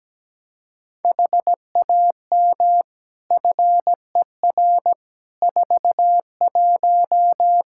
Morse code, heard as a keyed tone, about 17 wpm, 700 Hz. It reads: HAM FER 41
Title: Practice QSO - Exchange 3 - More Info - 17wpm